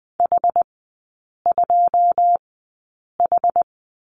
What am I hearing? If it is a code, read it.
Morse code, heard as a keyed tone, about 20 wpm, spaced as for 8 wpm, 700 Hz. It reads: H2H
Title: 081 Lesson 14e - Number 2 - Three Characters 20wpm